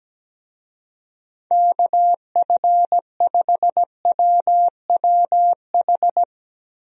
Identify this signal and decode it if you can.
Morse code, heard as a keyed tone, about 17 wpm, 700 Hz. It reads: KF5WWH
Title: USA Call Signs - 6 Characters - Repeated 2x - 17wpm